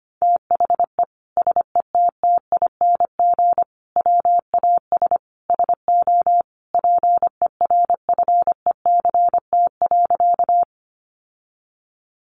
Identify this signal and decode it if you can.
Morse code, heard as a keyed tone, about 25 wpm, 700 Hz. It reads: THE SETTING WAS SO PERFECT.
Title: Sentences from Top 1300 Words 25wpm